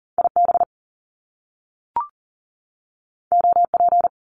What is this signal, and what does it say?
Morse code, heard as a keyed tone, about 39 wpm, 700 Hz. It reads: IB OP
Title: Top Two Letter Combinations in Words 40wpm